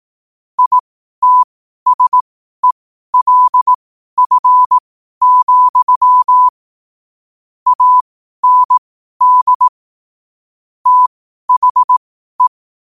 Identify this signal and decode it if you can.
Morse code, heard as a keyed tone, about 18 wpm, spaced as for 12 wpm, 1000 Hz. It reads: ITSELF, AND THE
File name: qotdfiles / 12wpm